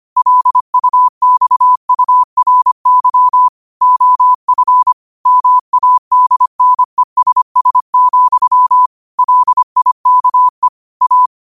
Morse code, heard as a keyed tone, about 25 wpm, 1000 Hz. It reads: LUXURY OF MADNESS, LIKE A
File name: qotdfiles / 25wpm